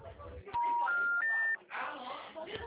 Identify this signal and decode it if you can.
Special information tone, with speech.
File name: out- -2027